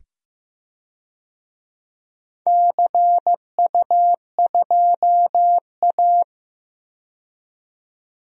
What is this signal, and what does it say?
Morse code, heard as a keyed tone, about 15 wpm, 700 Hz. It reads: CU2A
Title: DXCC-100 Call Signs - 4 Characters - 15wpm